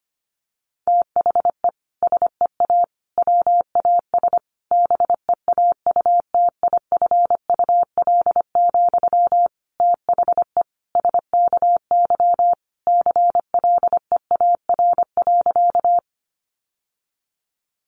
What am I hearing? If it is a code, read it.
Morse code, heard as a keyed tone, about 25 wpm, 700 Hz. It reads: THE SEA WAS BEAUTIFUL, THE SKY CLEAR.
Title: Sentences from Top 1200 Words 25wpm